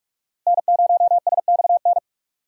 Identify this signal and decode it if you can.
Morse code, heard as a keyed tone, about 46 wpm, 700 Hz. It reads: N0SXN
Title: USA Call Signs - 5 Characters - Repeated 2x - 45wpm